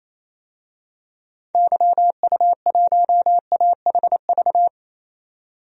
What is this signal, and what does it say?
Morse code, heard as a keyed tone, about 28 wpm, 700 Hz. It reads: YU1AHV
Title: DXCC-100 Call Signs - 6 Characters - 28wpm